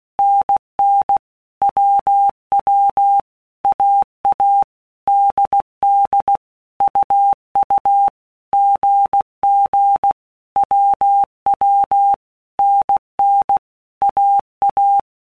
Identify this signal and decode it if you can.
Morse code, heard as a keyed tone, about 16 wpm, 800 Hz. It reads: NN WW AA DD UU GG WW NN AA
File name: All Groups 1-minute Blocks / Groups@15 / AWUNDG